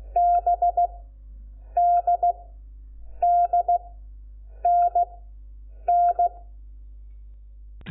Morse code, heard as a keyed tone, about 16 wpm, spaced as for 7 wpm, 700 Hz. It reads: BDDNN
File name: Lecciones / CLASE Nº5